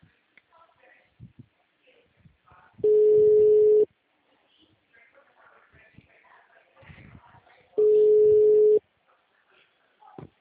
Ringback tone (Europe, 425 Hz).